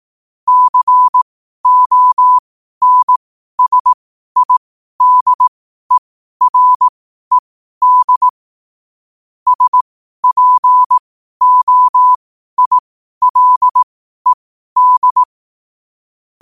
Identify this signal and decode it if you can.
Morse code, heard as a keyed tone, about 18 wpm, spaced as for 12 wpm, 1000 Hz. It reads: CONSIDERED SPOILED
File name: qotdfiles / 12wpm